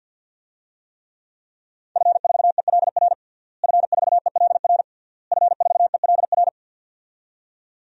Morse code, heard as a keyed tone, about 52 wpm, 700 Hz. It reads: F4ELR F4ELR F4ELR
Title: DXCC-100 Call Signs - 5 Characters - Repeated 3x - 50wpm